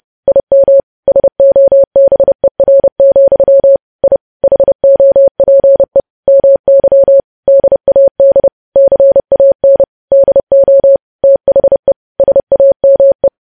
Morse code, heard as a keyed tone, about 30 wpm, 550 Hz. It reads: IM SOBER, I HOPE MY DAD CAN DO THE SAME
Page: news headlines in morse code at 30 wpm